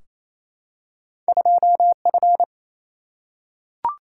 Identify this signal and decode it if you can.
Morse code, heard as a keyed tone, about 28 wpm, 700 Hz. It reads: 2F
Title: Single-Letter Single-Number 28wpm